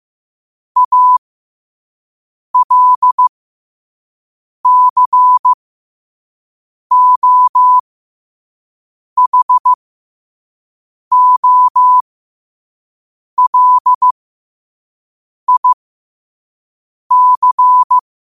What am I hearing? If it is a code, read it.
Morse code, heard as a keyed tone, about 15 wpm, spaced as for 5 wpm, 1000 Hz. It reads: ALCOHOLIC
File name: qotdfiles / 5wpm